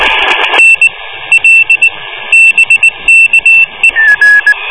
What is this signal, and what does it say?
Morse code, heard as a keyed tone, about 19 wpm, 2700 Hz. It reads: N L BKE